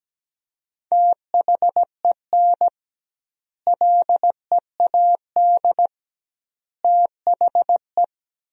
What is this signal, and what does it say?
Morse code, heard as a keyed tone, about 17 wpm, 700 Hz. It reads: THEN LEAD THE
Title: 2x Word-Spacing - Sentences from Top 500 Words - 17wpm